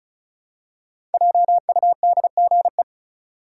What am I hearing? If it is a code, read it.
Morse code, heard as a keyed tone, about 34 wpm, 700 Hz. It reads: JUDGE
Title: Five Letter Words - 35wpm